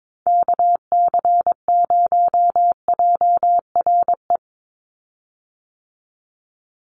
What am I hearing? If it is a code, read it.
Morse code, heard as a keyed tone, about 22 wpm, 700 Hz. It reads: KC0JRE
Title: USA Call Signs 22wpm